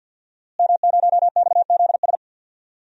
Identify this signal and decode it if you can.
Morse code, heard as a keyed tone, about 51 wpm, 700 Hz. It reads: M0XZS
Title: DXCC-100 Call Signs - 5 Characters - Repeated 2x - 50wpm